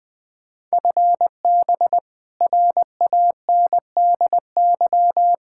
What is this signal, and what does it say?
Morse code, heard as a keyed tone, about 21 wpm, 700 Hz. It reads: FB RANDY
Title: Practice QSO - Exchange 3 - More Info - 20wpm